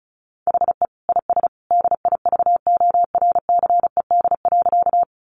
Morse code, heard as a keyed tone, about 34 wpm, 700 Hz. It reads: HE IS DIVORCED.
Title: Additional Easy Phrases 35wpm